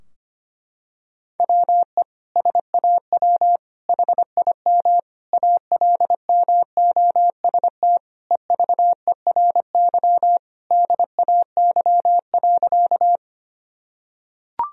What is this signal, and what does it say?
Morse code, heard as a keyed tone, about 25 wpm, 700 Hz. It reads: WE SAW HIM ALMOST EVERY DAY.
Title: Sentences from Top 1000 Words - Review Part 4 - 25wpm